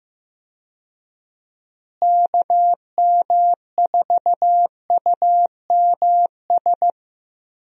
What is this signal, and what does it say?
Morse code, heard as a keyed tone, about 15 wpm, 700 Hz. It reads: KM4UMS
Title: USA Call Signs - 6 Characters - Repeated 2x - 15wpm